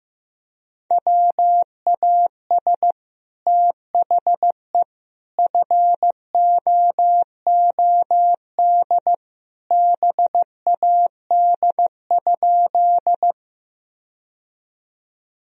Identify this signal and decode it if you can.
Morse code, heard as a keyed tone, about 15 wpm, 700 Hz. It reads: WAS THE FOOD BAD?
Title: Sentences from Top 1000 Words - Review Part 1 - 15wpm